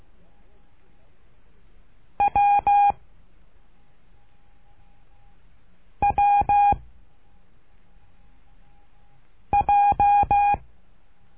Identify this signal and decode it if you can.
Morse code, heard as a keyed tone, about 16 wpm, 810 Hz. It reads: W W J
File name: cursos / CURSO DE TELEGRAFIA / Telegrafia - Trilha 12